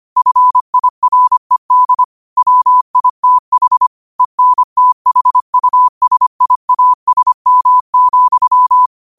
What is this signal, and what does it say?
Morse code, heard as a keyed tone, about 25 wpm, 1000 Hz. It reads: FIRED WITH ENTHUSIASM,